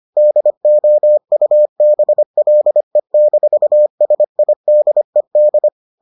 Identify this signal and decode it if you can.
Morse code, heard as a keyed tone, about 25 wpm, 600 Hz. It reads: DOUBLE-SIDED